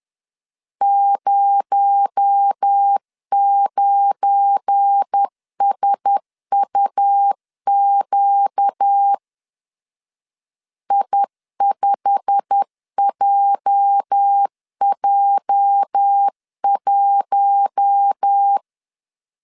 Morse code, heard as a keyed tone, about 10 wpm, 800 Hz. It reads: 09SUQ I5JJ1
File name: CW / DET / 1 9 1500C S C 10wpm 4557